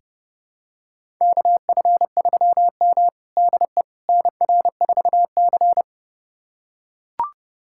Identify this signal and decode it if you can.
Morse code, heard as a keyed tone, about 30 wpm, 700 Hz. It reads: KF3M DE NR4C
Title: USA Callsign DE Callsign - 4 Characters - 30wpm